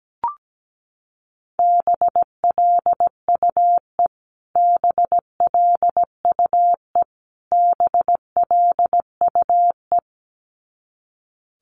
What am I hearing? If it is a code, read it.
Morse code, heard as a keyed tone, about 17 wpm, 700 Hz. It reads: BLUE BLUE BLUE
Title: Top 500 Words Repeated 17wpm